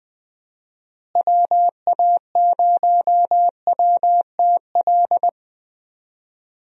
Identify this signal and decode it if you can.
Morse code, heard as a keyed tone, about 20 wpm, 700 Hz. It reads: WA0WTL
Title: USA Call Signs - 6 Characters - Repeated 2x - 20wpm